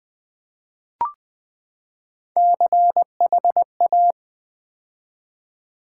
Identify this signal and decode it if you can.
Morse code, heard as a keyed tone, about 20 wpm, 700 Hz. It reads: CHA
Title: Build A Word - 3 Letters At A Time - 6-Letter Words - 20wpm